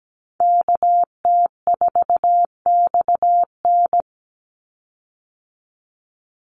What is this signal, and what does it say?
Morse code, heard as a keyed tone, about 17 wpm, 700 Hz. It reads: KT4XN
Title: Call Signs Repeated 3x - 17wpm